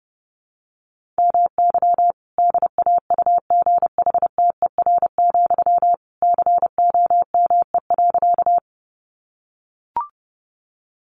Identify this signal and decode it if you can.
Morse code, heard as a keyed tone, about 30 wpm, 700 Hz. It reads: MY DAUGHTER, COME.
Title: Sentences from Top 1500 Words 30wpm